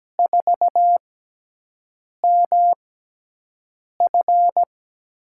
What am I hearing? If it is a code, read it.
Morse code, heard as a keyed tone, about 17 wpm, spaced as for 6 wpm, 700 Hz. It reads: 4MF
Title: USA Call Signs - 4 Characters - 6x Character Spacing - 17wpm